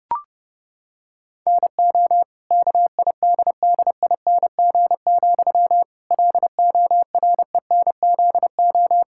Morse code, heard as a keyed tone, about 32 wpm, 700 Hz. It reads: NO KIDDING, LORENZO
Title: Pangram Challenge v2 - 30wpm